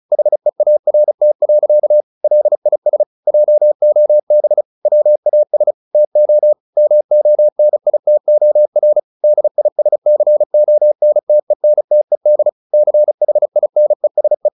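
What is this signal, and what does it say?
Morse code, heard as a keyed tone, about 35 wpm, 600 Hz. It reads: HEART. LIS JOB WAS TO MONITOR DISCONTENTED CHINESE
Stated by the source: Morse practice at 35 wpm from CW med Gnister, sending Ken Follett's novel Never in English.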